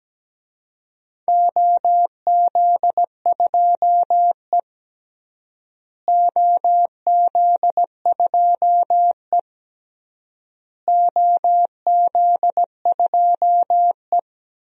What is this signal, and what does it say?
Morse code, heard as a keyed tone, about 17 wpm, 700 Hz. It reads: OZ2E OZ2E OZ2E
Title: DXCC-100 Call Signs - 4 Characters - Repeated 3x - 17wpm